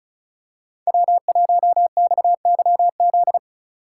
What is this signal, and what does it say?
Morse code, heard as a keyed tone, about 36 wpm, 700 Hz. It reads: W1XYZ
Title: USA Call Signs - 5 Characters - 2x Character Spacing - 35wpm